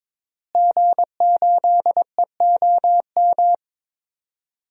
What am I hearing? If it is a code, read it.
Morse code, heard as a keyed tone, about 21 wpm, 700 Hz. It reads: G8EOM
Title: DXCC-100 Call Signs - 5 Characters - Repeated 2x - 22wpm